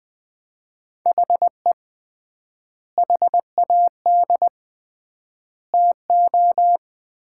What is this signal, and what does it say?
Morse code, heard as a keyed tone, about 20 wpm, 700 Hz. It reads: HE HAD TO
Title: 3x Word-Spacing - Sets of 3 Words - 20wpm